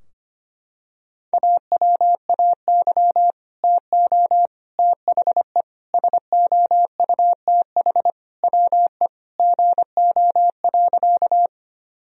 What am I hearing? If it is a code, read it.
Morse code, heard as a keyed tone, about 25 wpm, 700 Hz. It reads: AWAY TO THE SOUTH WE GO.
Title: Sentences from Top 1000 Words - Review Part 2 - 25wpm